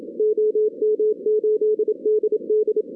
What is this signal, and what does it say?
Morse code, heard as a keyed tone, about 26 wpm, 440 Hz. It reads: OM8DD
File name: 2016 arrl dx cw / 2016-ARRL-DX-CW-K5TR- - 0-41